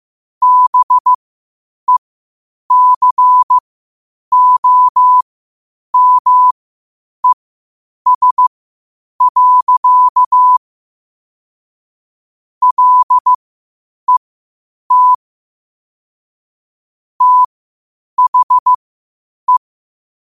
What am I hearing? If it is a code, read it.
Morse code, heard as a keyed tone, about 15 wpm, spaced as for 8 wpm, 1000 Hz. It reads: BECOMES. LET THE
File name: qotdfiles / 8wpm